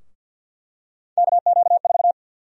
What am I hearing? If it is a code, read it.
Morse code, heard as a keyed tone, about 48 wpm, 700 Hz. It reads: KQ4
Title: USA Call Sign Prefixes - Repeated 2x - 50wpm